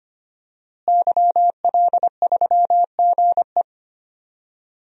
Morse code, heard as a keyed tone, about 25 wpm, 700 Hz. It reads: YL3GE
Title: DXCC-100 Call Signs - 5 Characters - Repeated 3x - 25wpm